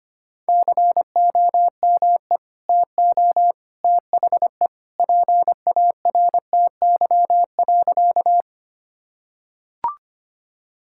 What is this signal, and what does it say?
Morse code, heard as a keyed tone, about 25 wpm, 700 Hz. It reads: COME TO THE PARTY.
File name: practice / 25wpm / Sentences with 2-Word Phrases - 25wpm